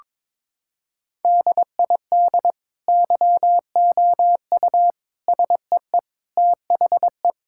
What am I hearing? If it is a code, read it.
Morse code, heard as a keyed tone, about 22 wpm, 700 Hz. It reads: DID YOU SEE THE
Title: Sentences from Top 1000 Words - Review Part 6 - 22wpm